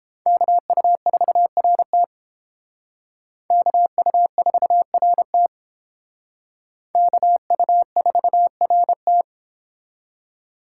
Morse code, heard as a keyed tone, about 31 wpm, 700 Hz. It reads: KU4RT KU4RT KU4RT